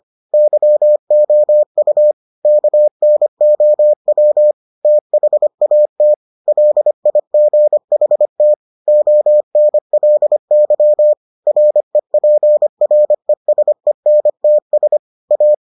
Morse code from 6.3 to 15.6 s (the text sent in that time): LIGHT ONLY REPRESENTS A